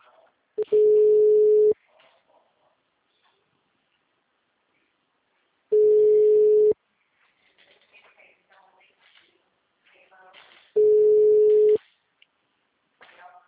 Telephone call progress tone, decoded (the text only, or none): ringback tone